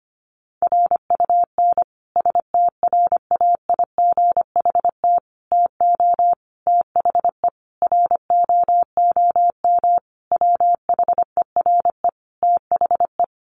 Morse code, heard as a keyed tone, about 25 wpm, 700 Hz. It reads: RUN STRAIGHT TO THE ROOM WHERE THE
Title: Sentences from Top 1000 Words 25wpm